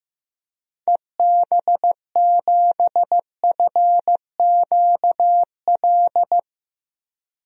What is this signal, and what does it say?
Morse code, heard as a keyed tone, about 15 wpm, 700 Hz. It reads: EB7FQL
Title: DXCC-100 Call Signs - 6 Characters - 15wpm